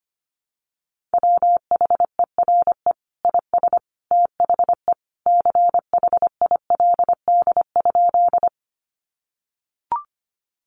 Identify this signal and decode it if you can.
Morse code, heard as a keyed tone, about 25 wpm, 700 Hz. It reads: WHERE IS THE CHILD?
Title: Sentences from Top 800 Words 25wpm